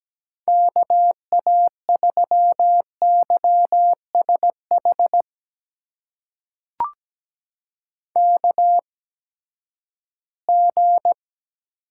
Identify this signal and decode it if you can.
Morse code, heard as a keyed tone, about 17 wpm, spaced as for 12 wpm, 700 Hz. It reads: KA3YSH KG
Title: USA Call Signs - 6 Characters - 8x Character Spacing - 17wpm